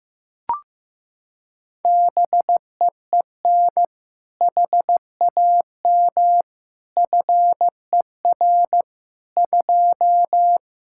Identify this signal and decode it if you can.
Morse code, heard as a keyed tone, about 15 wpm, 700 Hz. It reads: BEEN HAM FER 2
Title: Practice QSO - Exchange 3 - More Info - 15wpm